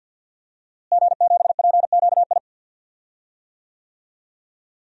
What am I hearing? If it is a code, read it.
Morse code, heard as a keyed tone, about 53 wpm, 700 Hz. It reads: G7PQI
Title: DXCC-100 Call Signs - 5 Characters - Repeated 2x - 50wpm